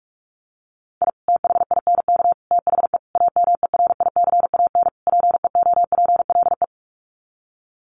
Morse code, heard as a keyed tone, about 44 wpm, 700 Hz. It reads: I THINK THE AMERICAN PEOPLE
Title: Sets of 5 Words Encore 45wpm